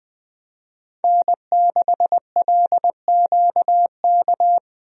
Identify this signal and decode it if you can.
Morse code, heard as a keyed tone, about 20 wpm, 700 Hz. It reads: N6LQK
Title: USA Call Signs - 5 Characters - 20wpm